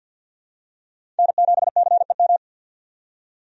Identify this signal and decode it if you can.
Morse code, heard as a keyed tone, about 50 wpm, 700 Hz. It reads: N7CEM
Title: USA Call Signs - 5 Characters - 50wpm